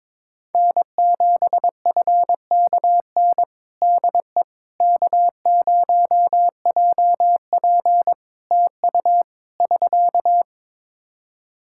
Morse code, heard as a keyed tone, about 22 wpm, 700 Hz. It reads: N7FKN DE K0JP TU <SK>
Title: Practice QSO - Exchange 4 - Ending - 22wpm